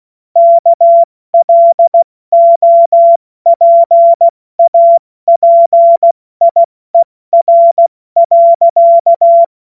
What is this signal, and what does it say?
Morse code, heard as a keyed tone, about 16 wpm, spaced as for 14 wpm, 680 Hz. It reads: KLOPAPIER.
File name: Morsekurs / Witze / Witz 29. Juni 2021, 70 F80